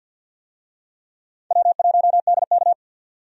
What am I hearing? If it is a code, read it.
Morse code, heard as a keyed tone, about 52 wpm, 700 Hz. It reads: W1DK
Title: USA Call Signs - 4 Characters - 50wpm